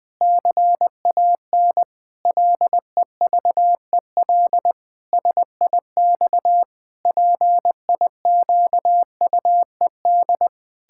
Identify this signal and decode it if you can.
Morse code, heard as a keyed tone, about 20 wpm, 700 Hz. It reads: CAN LEVEL SIX PIQUED
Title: Pangram Challenge v2 - 20wpm